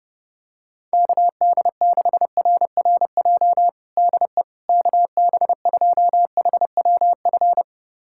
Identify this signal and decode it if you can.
Morse code, heard as a keyed tone, about 32 wpm, 700 Hz. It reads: KD6RRJ DE KB2HWF
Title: USA Callsign DE Callsign - 6 Characters - 30wpm